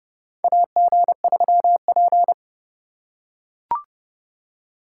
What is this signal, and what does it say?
Morse code, heard as a keyed tone, about 31 wpm, 700 Hz. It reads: AG3P
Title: USA Call Signs - 2x Character Spacing - 30wpm